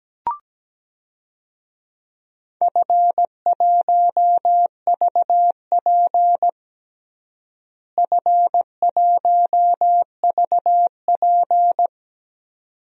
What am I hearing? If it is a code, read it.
Morse code, heard as a keyed tone, about 17 wpm, 700 Hz. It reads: F1VP F1VP
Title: DXCC-100 Call Signs - 4 Characters - Repeated 2x - 17wpm